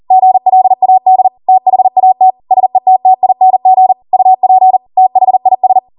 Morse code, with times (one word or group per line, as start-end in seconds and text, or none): 0.1-1.3 GLAD
1.5-2.3 THAT
2.5-3.9 SETTING
4.1-4.8 UP
5.0-5.8 THIS